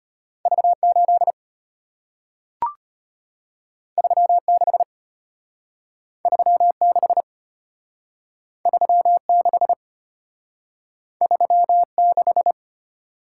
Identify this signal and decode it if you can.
Morse code, heard as a keyed tone, about 34 wpm, 700 Hz. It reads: V8 36 36 36 36